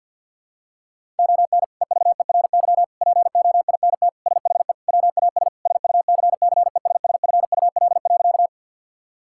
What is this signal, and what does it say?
Morse code, heard as a keyed tone, about 52 wpm, 700 Hz. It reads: ON EVERY POINT SHE WAS SUCCESSFUL.
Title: Sentences from Top 1000 Words - Review Part 6 - 50wpm